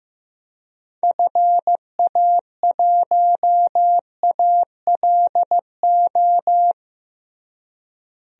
Morse code, heard as a keyed tone, about 15 wpm, 700 Hz. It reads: FA1ALO
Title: DXCC-100 Call Signs - 6 Characters - Repeated 2x - 15wpm